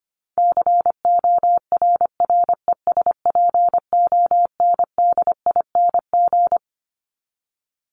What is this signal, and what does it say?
Morse code, heard as a keyed tone, about 25 wpm, 700 Hz. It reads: CORRESPONDING
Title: ING Suffix - Top 100 Words with Base Words 25wpm